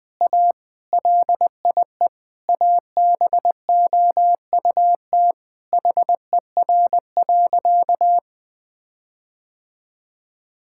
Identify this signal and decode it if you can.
Morse code, heard as a keyed tone, about 20 wpm, 700 Hz. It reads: A LIE ABOUT HER.
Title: Sentences from Top 1600 Words 20wpm